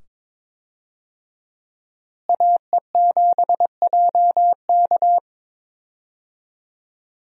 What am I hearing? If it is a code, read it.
Morse code, heard as a keyed tone, about 22 wpm, 700 Hz. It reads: AE7JK
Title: USA Call Signs - 5 Characters - Repeated 2x - 22wpm